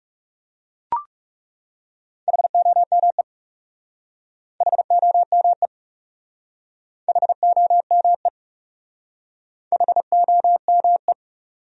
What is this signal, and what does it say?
Morse code, heard as a keyed tone, about 39 wpm, 700 Hz. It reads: HOME HOME HOME HOME